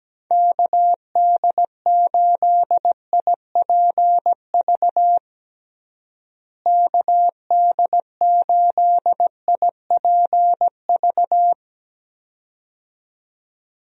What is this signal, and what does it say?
Morse code, heard as a keyed tone, about 17 wpm, 700 Hz. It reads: KD8IPV KD8IPV
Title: USA Call Signs - 6 Characters - Repeated 2x - 17wpm